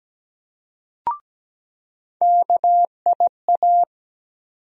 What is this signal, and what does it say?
Morse code, heard as a keyed tone, about 17 wpm, 700 Hz. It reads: KIA
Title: USA Call Sign Postfixes - 3 Letters 17wpm